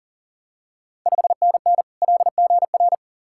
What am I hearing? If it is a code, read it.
Morse code, heard as a keyed tone, about 40 wpm, 690 Hz. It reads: 5NN LGR
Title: Ontario QSO Party Exchange 40wpm